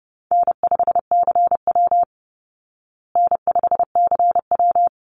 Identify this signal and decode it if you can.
Morse code, heard as a keyed tone, about 30 wpm, 700 Hz. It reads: N5CW N5CW
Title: Call Signs Repeated 3x - 30wpm